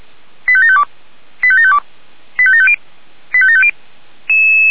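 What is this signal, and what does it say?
Five-tone selective call (ZVEI), heard as quick runs of five tones, tones of about 70 ms. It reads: 85752 85752 85750 85750
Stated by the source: ZVEI five-tone selective call received by SDR, from the feuerwehr-orlamuende.org fire brigade recording server.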